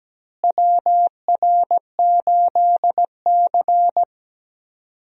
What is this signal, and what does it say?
Morse code, heard as a keyed tone, about 17 wpm, 700 Hz. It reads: WR8C
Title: USA Call Signs - 4 Characters - Repeated 2x - 17wpm